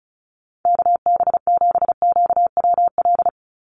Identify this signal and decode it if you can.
Morse code, heard as a keyed tone, about 36 wpm, 700 Hz. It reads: KB7QWL
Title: USA Call Signs 35wpm